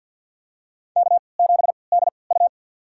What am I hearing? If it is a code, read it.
Morse code, heard as a keyed tone, about 51 wpm, spaced as for 28 wpm, 700 Hz. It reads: K7DU
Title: USA Call Signs - 4 Characters - 3x Character Spacing - 50wpm